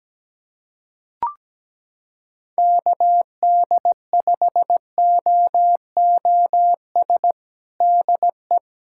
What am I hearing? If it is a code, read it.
Morse code, heard as a keyed tone, about 17 wpm, 700 Hz. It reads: KD5OOS DE
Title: USA Callsign DE Callsign 17wpm